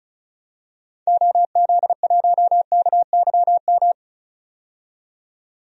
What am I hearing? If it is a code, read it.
Morse code, heard as a keyed tone, about 36 wpm, 700 Hz. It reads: OZ1KYM
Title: DXCC-100 Call Signs - 6 Characters - 35wpm